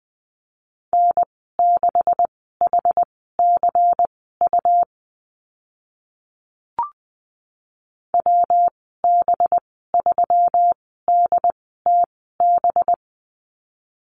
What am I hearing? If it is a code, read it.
Morse code, heard as a keyed tone, about 20 wpm, spaced as for 14 wpm, 700 Hz. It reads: N6HCU WB3DTB
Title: 193 Lesson 31g - Letter v - Call Signs 20wpm